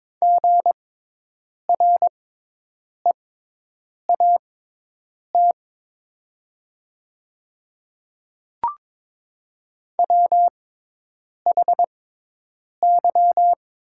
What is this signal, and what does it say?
Morse code, heard as a keyed tone, about 22 wpm, spaced as for 8 wpm, 700 Hz. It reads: GREAT WHY
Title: 6x Character Spacing - Top 200 Words - 22wpm